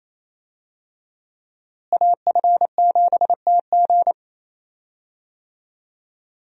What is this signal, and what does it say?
Morse code, heard as a keyed tone, about 28 wpm, 700 Hz. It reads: AF7TG
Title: USA Call Signs - 5 Characters - Repeated 3x - 28wpm